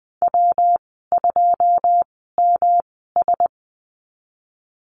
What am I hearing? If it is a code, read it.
Morse code, heard as a keyed tone, about 20 wpm, spaced as for 15 wpm, 700 Hz. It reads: W2MS
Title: 122 Lesson 20g - Number 3 - Call Signs 20wpm